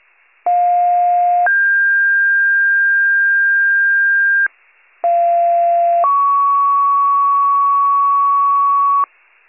Two-tone page, heard as a one-second tone, then a three-second tone, about 710 then 1600 Hz, about 690 then 1120 Hz, with noise. It is heard from 0.5 to 9.1 s.
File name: pager / 21 RESCUE / wcalert